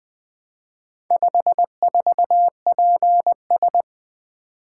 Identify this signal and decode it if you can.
Morse code, heard as a keyed tone, about 20 wpm, 700 Hz. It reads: 54PS
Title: Four Letter-Number Combinations - 20wpm